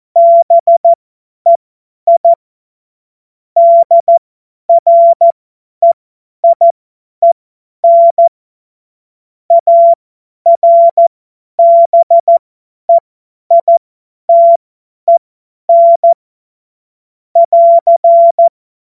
Morse code, heard as a keyed tone, about 14 wpm, spaced as for 10 wpm, 680 Hz. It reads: BEI DREIEN ARBEITEN <AR>
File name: Morsekurs / Witze / Witz 10 Juni 2025 50 F70